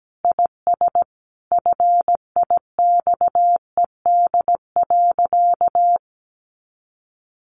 Morse code, heard as a keyed tone, about 17 wpm, 700 Hz. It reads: IS FIXED.
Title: Sentences from Top 1100 Words 17wpm